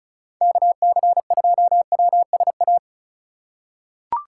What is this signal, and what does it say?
Morse code, heard as a keyed tone, about 34 wpm, 700 Hz. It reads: KC2WSA
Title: USA Call Signs - 2x Character Spacing - 35wpm